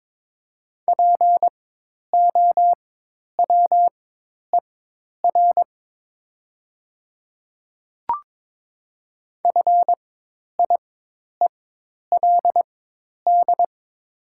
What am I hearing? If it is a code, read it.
Morse code, heard as a keyed tone, about 22 wpm, spaced as for 10 wpm, 700 Hz. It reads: POWER FIELD
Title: 4x Character Spacing - Top 500 Words - 22wpm